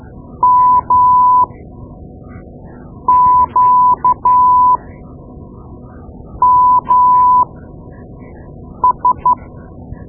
Morse code, heard as a keyed tone, about 13 wpm, spaced as for 5 wpm, 1000 Hz. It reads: MQMS
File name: Morse Lessons / Morse Lessons / Tape-2B